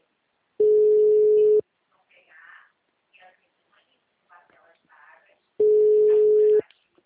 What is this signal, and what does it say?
Ringback tone (Europe, 425 Hz), with speech.